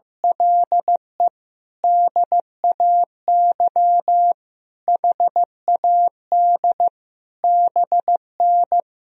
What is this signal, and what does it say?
Morse code, heard as a keyed tone, about 15 wpm, 700 Hz. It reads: LE DAY HAD BN
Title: Sentences from Top 500 Words and Abbreviations 15wpm